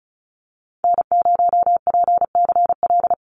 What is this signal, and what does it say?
Morse code, heard as a keyed tone, about 34 wpm, 700 Hz. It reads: N0PCL
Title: NA SOTA Activator Callsigns 35wpm